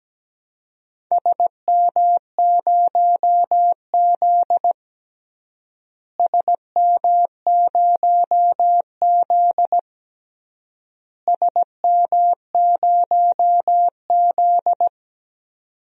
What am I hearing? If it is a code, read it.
Morse code, heard as a keyed tone, about 17 wpm, 700 Hz. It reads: SM0Z SM0Z SM0Z
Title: DXCC-100 Call Signs - 4 Characters - Repeated 3x - 17wpm